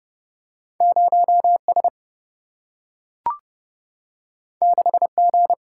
Morse code, heard as a keyed tone, about 30 wpm, 700 Hz. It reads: 0S 6G
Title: Single-Letter Single-Number 30wpm